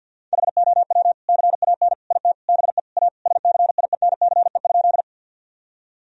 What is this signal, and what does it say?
Morse code, heard as a keyed tone, about 49 wpm, 700 Hz. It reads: HOW CAN IT BE A SCIENCE?
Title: Sentences from Top 1000 Words - Review Part 2 - 50wpm